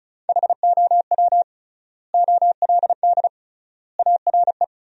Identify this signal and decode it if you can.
Morse code, heard as a keyed tone, about 36 wpm, 700 Hz. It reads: HOW OLD ARE
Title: 3x Word-Spacing - Sentences from Top 200 Words - 35wpm